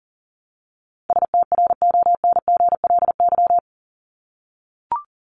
Morse code, heard as a keyed tone, about 40 wpm, 700 Hz. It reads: STRONGLY
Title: LY Suffix - Top 100 Words with Base Words 40wpm